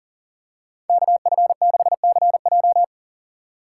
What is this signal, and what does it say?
Morse code, heard as a keyed tone, about 38 wpm, 700 Hz. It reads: KF6CJ
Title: USA Call Signs - 3x Character Spacing - 40wpm